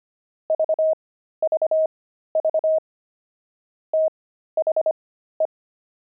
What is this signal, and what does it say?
Morse code, heard as a keyed tone, about 25 wpm, spaced as for 13 wpm, 630 Hz. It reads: VVV THE
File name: pangram-fs-13-cs-25-wizards